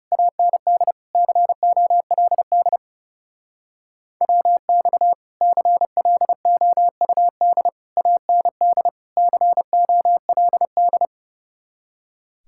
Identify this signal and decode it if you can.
Morse code, heard as a keyed tone, about 39 wpm, 700 Hz. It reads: AND COLD WX CLOUD AND COLD